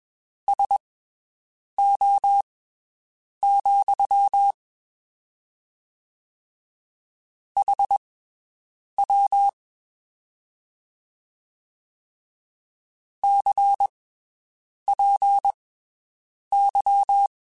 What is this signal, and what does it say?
Morse code, heard as a keyed tone, about 21 wpm, spaced as for 7 wpm, 800 Hz. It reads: SO, HW CPY
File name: (spaced) / QSOs Spaced-mp3s / WD0BGP 1 @20 sp